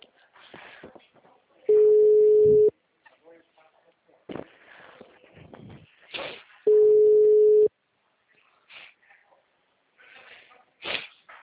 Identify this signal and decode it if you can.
Ringback tone (Europe, 425 Hz), with speech.
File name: out- -2005- - - .57711